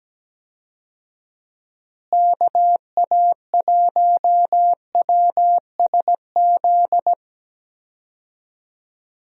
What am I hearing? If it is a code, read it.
Morse code, heard as a keyed tone, about 17 wpm, 700 Hz. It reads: KA1WSZ
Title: USA Call Signs - 6 Characters - Repeated 2x - 17wpm